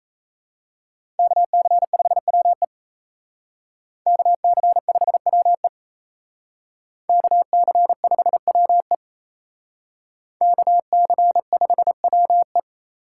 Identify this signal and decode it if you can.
Morse code, heard as a keyed tone, about 35 wpm, 700 Hz. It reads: KC5WE KC5WE KC5WE KC5WE